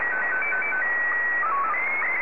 SSTV (slow-scan television), Scottie 1.